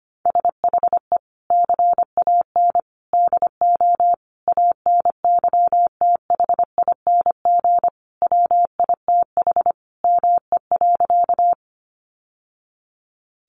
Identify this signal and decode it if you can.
Morse code, heard as a keyed tone, about 25 wpm, 700 Hz. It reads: SHE CAN DO ANYTHING WITH ME.
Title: Sentences from Top 1000 Words 25wpm